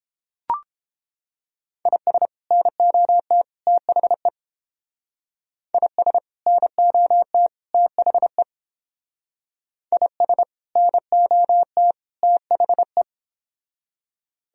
Morse code, heard as a keyed tone, about 29 wpm, 700 Hz. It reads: IS NOT THE IS NOT THE IS NOT THE